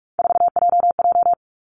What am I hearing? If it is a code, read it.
Morse code, heard as a keyed tone, about 46 wpm, 700 Hz. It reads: 4JJ
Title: Three Challenging Letter-Number Combinations 45wpm